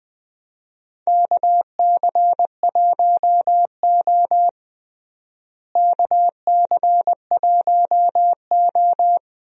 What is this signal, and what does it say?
Morse code, heard as a keyed tone, about 20 wpm, 700 Hz. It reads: KC1O KC1O
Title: USA Call Signs - 4 Characters - Repeated 2x - 20wpm